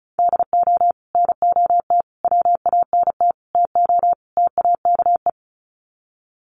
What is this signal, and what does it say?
Morse code, heard as a keyed tone, about 34 wpm, 700 Hz. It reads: DO NOT WANT TO TAKE
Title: Sets of 5 Words Encore2 35wpm